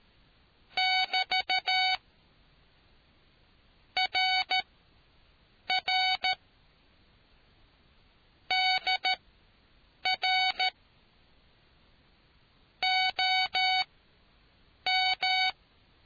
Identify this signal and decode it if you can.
Morse code, heard as a keyed tone, about 13 wpm, spaced as for 6 wpm, 750 Hz. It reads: <BT> RR DR OM